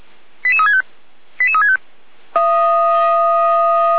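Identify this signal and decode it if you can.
Five-tone selective call (ZVEI), heard as quick runs of five tones, tones of about 70 ms. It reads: 88365 88365
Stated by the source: ZVEI five-tone selective call 88459 received on SDR.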